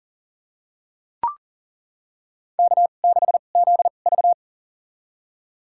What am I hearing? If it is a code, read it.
Morse code, heard as a keyed tone, about 40 wpm, spaced as for 29 wpm, 700 Hz. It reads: K6ZV
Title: USA Call Signs - 4 Characters - 2x Character Spacing - 40wpm